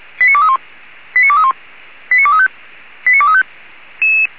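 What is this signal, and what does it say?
Five-tone selective call (ZVEI), heard as quick runs of five tones, tones of about 70 ms. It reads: 89231 89231 89235 89235
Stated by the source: ZVEI five-tone selective call received with an SDR.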